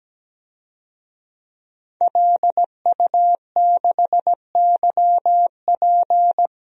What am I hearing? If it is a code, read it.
Morse code, heard as a keyed tone, about 17 wpm, 700 Hz. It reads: LU6YP
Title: DXCC-100 Call Signs - 5 Characters - 17wpm